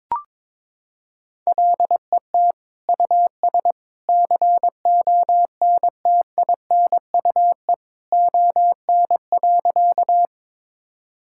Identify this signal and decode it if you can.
Morse code, heard as a keyed tone, about 22 wpm, 700 Hz. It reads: LET US CONTINUE ON.
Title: Sentences from Top 1000 Words - Review Part 6 - 22wpm